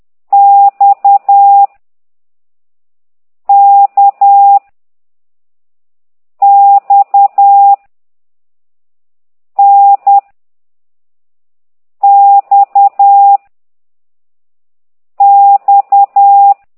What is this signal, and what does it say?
Morse code, heard as a keyed tone, about 10 wpm, 800 Hz. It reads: X K X N X X
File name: exerc7